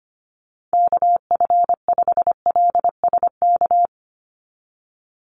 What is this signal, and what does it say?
Morse code, heard as a keyed tone, about 25 wpm, 700 Hz. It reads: KF5LSK
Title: USA Call Signs 25wpm